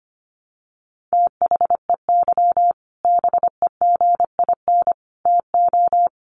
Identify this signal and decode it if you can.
Morse code, heard as a keyed tone, about 25 wpm, 700 Hz. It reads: THEY BEGIN TO
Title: Sentences from Top 1200 Words 25wpm